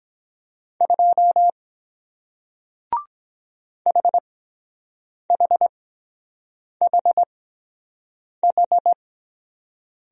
Morse code, heard as a keyed tone, about 23 wpm, spaced as for 7 wpm, 700 Hz. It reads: 2 HHHH